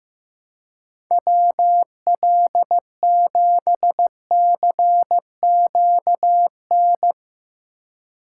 Morse code, heard as a keyed tone, about 16 wpm, 700 Hz. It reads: WL7CQN
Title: USA Call Signs - 6 Characters - 3x Character Spacing - 15wpm